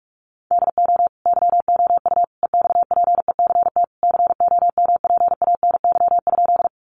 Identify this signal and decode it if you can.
Morse code, heard as a keyed tone, about 44 wpm, 700 Hz. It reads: DO YOU EXPECT COMPANY?
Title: Sentences from Top 1100 Words 45wpm